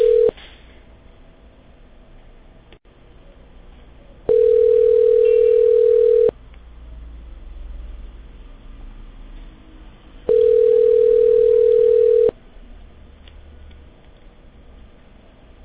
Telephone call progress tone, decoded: ringback tone